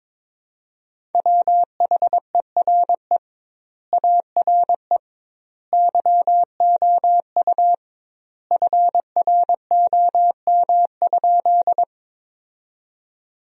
Morse code, heard as a keyed tone, about 22 wpm, 700 Hz. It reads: WHERE ARE YOU FROM?
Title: 2x Word-Spacing - Sentences from Top 200 Words - 22wpm